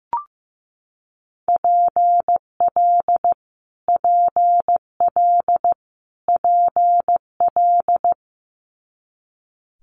Morse code, heard as a keyed tone, about 15 wpm, 700 Hz. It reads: PL PL PL
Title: Multi-letter Phonemes Repeated 3x 15wpm